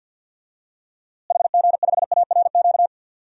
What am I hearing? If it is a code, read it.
Morse code, heard as a keyed tone, about 47 wpm, 700 Hz. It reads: HG5ARQ
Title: DXCC-100 Call Signs - 6 Characters - Repeated 2x - 50wpm